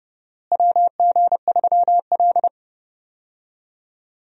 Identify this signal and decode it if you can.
Morse code, heard as a keyed tone, about 29 wpm, 700 Hz. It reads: WG3L
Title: USA Call Signs - 4 Characters - Repeated 3x - 30wpm